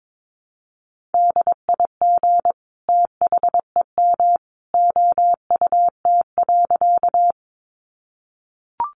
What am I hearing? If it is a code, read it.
Morse code, heard as a keyed tone, about 22 wpm, 700 Hz. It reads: DIG THEM OUT.
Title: A Sound Of Thunder 22wpm